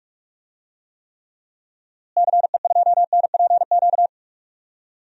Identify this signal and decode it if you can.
Morse code, heard as a keyed tone, about 46 wpm, 700 Hz. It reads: CE2NPQ
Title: DXCC-100 Call Signs - 6 Characters - Repeated 3x - 45wpm